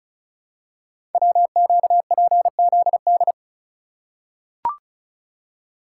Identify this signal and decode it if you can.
Morse code, heard as a keyed tone, about 36 wpm, 700 Hz. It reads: WQPZD
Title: Code Groups - 5 Letter:Numbers - 35wpm